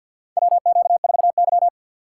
Morse code, heard as a keyed tone, about 51 wpm, 700 Hz. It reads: WQ4Y
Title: USA Call Signs - 3x Character Spacing - 50wpm